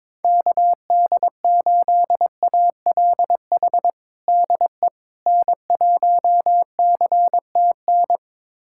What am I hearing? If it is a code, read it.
Morse code, heard as a keyed tone, about 22 wpm, 700 Hz. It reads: KD8ALH DE N1CTN
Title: USA Callsign DE Callsign 22wpm